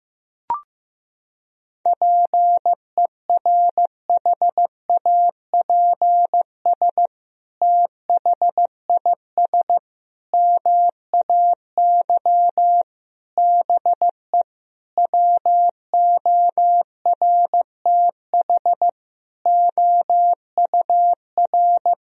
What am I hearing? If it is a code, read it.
Morse code, heard as a keyed tone, about 16 wpm, 700 Hz. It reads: PERHAPS THIS MAY BE WORTH OUR
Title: Sentences from Top 1000 Words - Review Part 2 - 15wpm